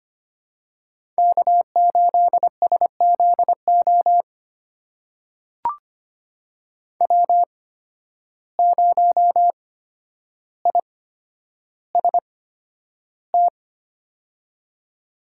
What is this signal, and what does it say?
Morse code, heard as a keyed tone, about 25 wpm, spaced as for 11 wpm, 700 Hz. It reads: K8SZO W0IST
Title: USA Call Signs - 5 Characters - 8x Character Spacing - 25wpm